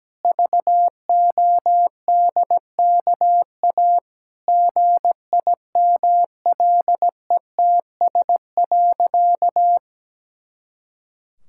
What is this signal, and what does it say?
Morse code, heard as a keyed tone, about 17 wpm, 700 Hz. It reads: VODKA GIMLETS.
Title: Pangram Challenge v2 - 17wpm